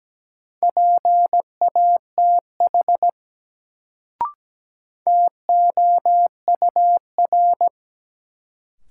Morse code, heard as a keyed tone, about 17 wpm, 700 Hz. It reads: PATH TOUR
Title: Sentences from Top 1400 Words - New Word Drill Alt - 17wpm